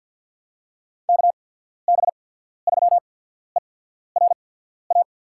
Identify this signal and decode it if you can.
Morse code, heard as a keyed tone, about 53 wpm, spaced as for 14 wpm, 700 Hz. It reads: KB3ERA
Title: USA Call Signs - 6 Characters - 8x Character Spacing - 50wpm